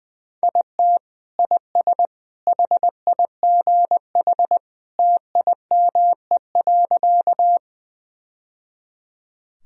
Morse code, heard as a keyed tone, about 20 wpm, 700 Hz. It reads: IT IS HIGH TIME.
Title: Sentences from Top 1000 Words - Review Part 1 - 20wpm